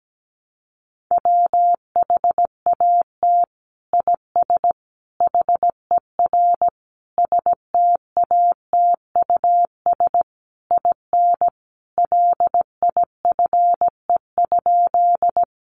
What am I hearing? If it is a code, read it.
Morse code, heard as a keyed tone, about 17 wpm, 700 Hz. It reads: WHAT IS HER STATUS IN LIFE?
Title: Sentences from Top 1400 Words 17wpm